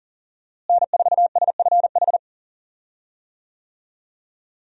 Morse code, heard as a keyed tone, about 40 wpm, 700 Hz. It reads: N4SFH
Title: USA Call Signs - 5 Characters - Repeated 2x - 40wpm